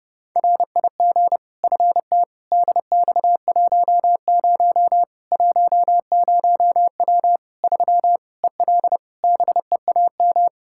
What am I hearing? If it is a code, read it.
Morse code, heard as a keyed tone, about 30 wpm, 700 Hz. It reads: RIG FT DX10 10W 3 EL BEAM
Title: K1USN SST QSO Practice 30wpm